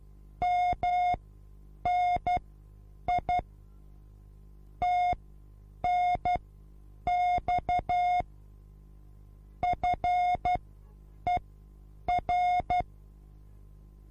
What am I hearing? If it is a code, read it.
Morse code, heard as a keyed tone, about 12 wpm, spaced as for 8 wpm, 720 Hz. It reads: MNI TNX FER